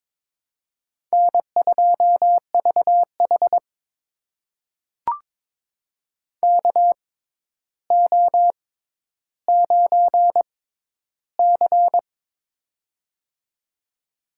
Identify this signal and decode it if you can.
Morse code, heard as a keyed tone, about 23 wpm, spaced as for 11 wpm, 700 Hz. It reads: N2VH KO9C
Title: USA Call Signs - 4 Characters - 6x Character Spacing - 22wpm